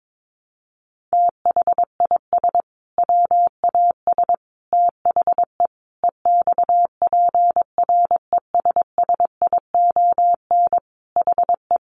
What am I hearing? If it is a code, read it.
Morse code, heard as a keyed tone, about 22 wpm, 700 Hz. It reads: THIS WAS THE EXPRESSION HE
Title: Sentences from Top 1200 Words 22wpm